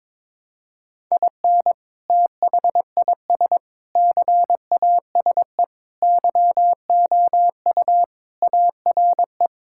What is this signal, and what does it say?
Morse code, heard as a keyed tone, about 22 wpm, 700 Hz. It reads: IN THIS CASE YOU ARE
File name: practice / 22wpm / Sentences with 3-Word Phrases - 22wpm